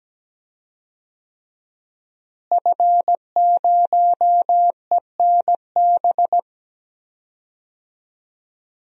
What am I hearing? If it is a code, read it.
Morse code, heard as a keyed tone, about 17 wpm, 700 Hz. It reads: F0ENB
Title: DXCC-100 Call Signs - 5 Characters - Repeated 3x - 17wpm